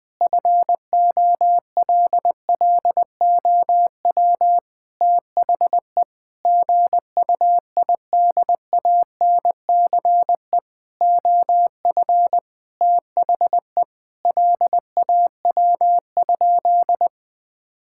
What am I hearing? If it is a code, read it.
Morse code, heard as a keyed tone, about 20 wpm, 700 Hz. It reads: FOLLOW THE GUIDANCE OF THE LAW?
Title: Sentences from Top 1600 Words 20wpm